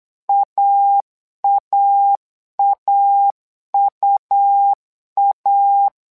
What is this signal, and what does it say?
Morse code, heard as a keyed tone, about 8 wpm, 800 Hz. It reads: AAAUA
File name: LEZIONI / 2 UHDN / 2 UH 8wpm 2129